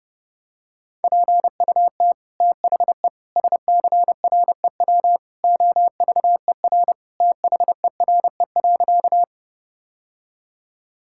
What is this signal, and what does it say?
Morse code, heard as a keyed tone, about 32 wpm, 700 Hz. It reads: PUT THE SCREW OVER THERE.
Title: Sentences from Top 1600 Words 30wpm